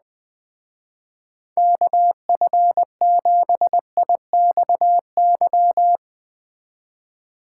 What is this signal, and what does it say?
Morse code, heard as a keyed tone, about 20 wpm, 700 Hz. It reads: KF7IXY